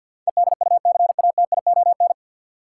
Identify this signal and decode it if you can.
Morse code, heard as a keyed tone, about 53 wpm, 700 Hz. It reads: EDUCATION
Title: Sentences from Top 1000 Words - Review Part 6 - New Word Drill - 50wpm